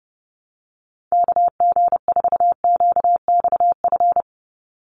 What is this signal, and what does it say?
Morse code, heard as a keyed tone, about 30 wpm, 700 Hz. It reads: KG4QXF
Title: USA Call Signs 30wpm